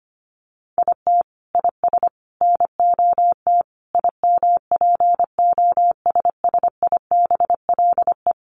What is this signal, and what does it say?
Morse code, heard as a keyed tone, about 25 wpm, 700 Hz. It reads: IT IS NOT IMPOSSIBLE
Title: Sentences from Top 1100 Words 25wpm